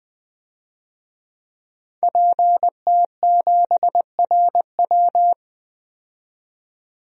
Morse code, heard as a keyed tone, about 20 wpm, 700 Hz. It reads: PT7RW
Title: DXCC-100 Call Signs - 5 Characters - 20wpm